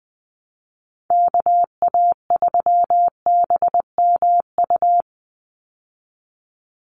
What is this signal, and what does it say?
Morse code, heard as a keyed tone, about 20 wpm, 700 Hz. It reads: KA3BMU
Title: Call Signs Repeated 2x - 20wpm